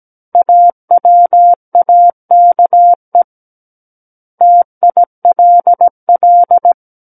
Morse code, heard as a keyed tone, about 17 wpm, 700 Hz. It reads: AWAKE TILL
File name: Book One / 08-Chapter 8 - 17 WPM